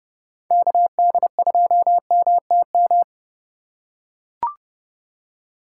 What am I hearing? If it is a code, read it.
Morse code, heard as a keyed tone, about 30 wpm, 700 Hz. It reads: KD2MTM
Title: USA Call Signs - 6 Characters - 2x Character Spacing - 30wpm